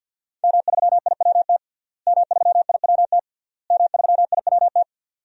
Morse code, heard as a keyed tone, about 50 wpm, 700 Hz. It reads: M3IWT M3IWT M3IWT
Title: DXCC-100 Call Signs - 5 Characters - Repeated 3x - 50wpm